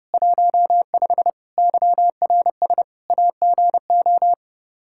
Morse code, heard as a keyed tone, about 32 wpm, 700 Hz. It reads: 15 YRS AGO
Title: Practice QSO - Exchange 3 - More Info - 30wpm